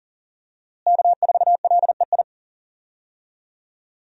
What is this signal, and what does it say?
Morse code, heard as a keyed tone, about 38 wpm, 700 Hz. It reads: K4LEI
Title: USA Call Signs - 5 Characters - Repeated 3x - 40wpm